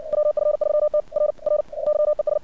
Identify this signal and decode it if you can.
Morse code, heard as a keyed tone, about 56 wpm, 600 Hz. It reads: WF3T R R YES
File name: archived / CRSnet